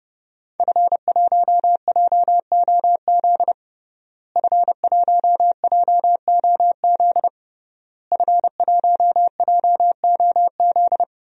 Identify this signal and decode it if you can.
Morse code, heard as a keyed tone, about 32 wpm, 700 Hz. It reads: F1JOZ F1JOZ F1JOZ
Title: DXCC-100 Call Signs - 5 Characters - Repeated 3x - 30wpm